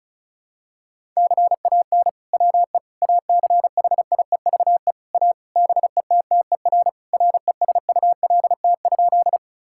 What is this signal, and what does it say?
Morse code, heard as a keyed tone, about 35 wpm, 700 Hz. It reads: CAN WE ACHIEVE A BETTER RESULT?
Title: Sentences from Top 1000 Words - Review Part 6 - 35wpm